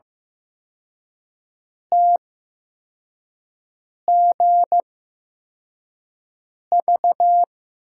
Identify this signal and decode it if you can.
Morse code, heard as a keyed tone, about 16 wpm, spaced as for 4 wpm, 700 Hz. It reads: TGV
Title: USA Call Signs - 8x Character Spacing - 15wpm